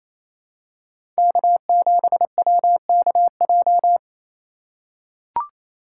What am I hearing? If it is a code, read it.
Morse code, heard as a keyed tone, about 27 wpm, 700 Hz. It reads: K7WKJ
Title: USA Call Signs - 6x Character Spacing - 28wpm